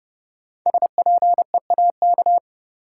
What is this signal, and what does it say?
Morse code, heard as a keyed tone, about 30 wpm, 700 Hz. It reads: SPEAK
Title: Sentences from Top 1000 Words - Review Part 2 - New Word Drill - 30wpm